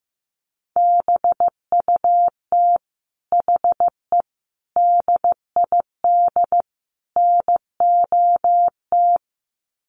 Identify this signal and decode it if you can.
Morse code, heard as a keyed tone, about 15 wpm, 700 Hz. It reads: BUT HE DID NOT
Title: Sets of 4 Words 15wpm